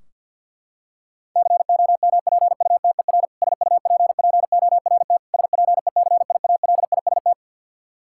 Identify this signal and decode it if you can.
Morse code, heard as a keyed tone, about 48 wpm, 700 Hz. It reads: COMPUTER SUPPORT SPECIALIST
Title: Practice QSO - Exchange 3 - More Info - 50wpm